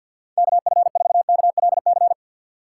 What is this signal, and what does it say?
Morse code, heard as a keyed tone, about 53 wpm, 700 Hz. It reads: KF4KLC
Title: USA Call Signs - 4x Character Spacing - 50wpm